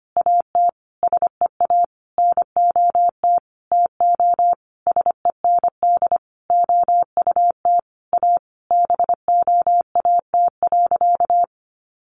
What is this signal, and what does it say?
Morse code, heard as a keyed tone, about 25 wpm, 700 Hz. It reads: AT SEA NOT TO SEND OUT A BOAT.
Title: Sentences from Top 1500 Words 25wpm